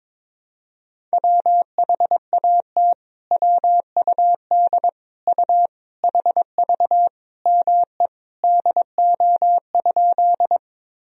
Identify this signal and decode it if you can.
Morse code, heard as a keyed tone, about 22 wpm, 700 Hz. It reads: WHAT WUD U HV ME DO?
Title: Sentences from Top 200 Words and Abbreviations 22wpm